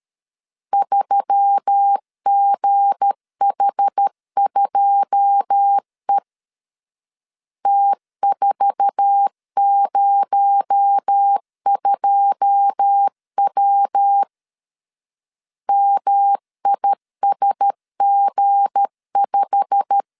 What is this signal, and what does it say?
Morse code, heard as a keyed tone, about 13 wpm, 800 Hz. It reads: 3GH2E T402W MISG5
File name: LEZIONI / 9 2 12wpm 2301